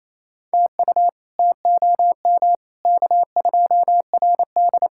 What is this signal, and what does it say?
Morse code, heard as a keyed tone, about 28 wpm, 700 Hz. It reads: TU TOM K2RD
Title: K1USN SST QSO Practice - 28wpm